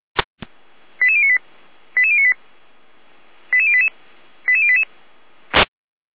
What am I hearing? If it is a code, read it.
Five-tone selective call (ZVEI), heard as quick runs of five tones, tones of about 70 ms. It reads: 88098 88098 88088 88088
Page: /455/SDR/ZVEI/